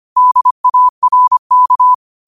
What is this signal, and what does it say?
Morse code, heard as a keyed tone, about 25 wpm, 1000 Hz. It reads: DARK
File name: qotdfiles / 25wpm